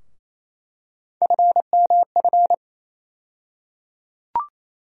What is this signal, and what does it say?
Morse code, heard as a keyed tone, about 28 wpm, 700 Hz. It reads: FMF
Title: USA Call Sign Postfixes - 3 Letters 28wpm